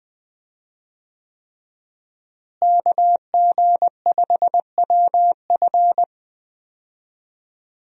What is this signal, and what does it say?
Morse code, heard as a keyed tone, about 19 wpm, 700 Hz. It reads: KG5WF
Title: USA Call Signs - 5 Characters - Repeated 3x - 20wpm